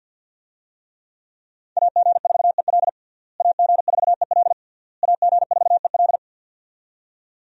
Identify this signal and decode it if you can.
Morse code, heard as a keyed tone, about 47 wpm, 700 Hz. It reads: AG4EL AG4EL AG4EL
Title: USA Call Signs - 5 Characters - Repeated 3x - 50wpm